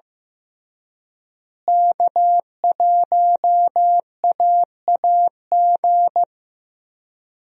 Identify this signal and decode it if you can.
Morse code, heard as a keyed tone, about 15 wpm, 700 Hz. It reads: K1AAG
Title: USA Call Signs - 5 Characters - Repeated 3x - 15wpm